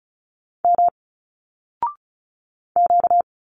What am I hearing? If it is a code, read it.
Morse code, heard as a keyed tone, about 36 wpm, 700 Hz. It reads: M Q
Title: Single Letters 35wpm